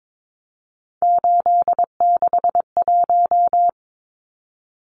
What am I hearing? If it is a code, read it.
Morse code, heard as a keyed tone, about 22 wpm, 700 Hz. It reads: 861
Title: Numbers - 3-Digit 22wpm